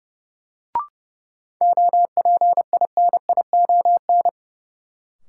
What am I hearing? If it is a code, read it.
Morse code, heard as a keyed tone, about 30 wpm, 700 Hz. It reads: OPINION
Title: Sentences from Top 1200 Words - New Word Drill Alt - 30wpm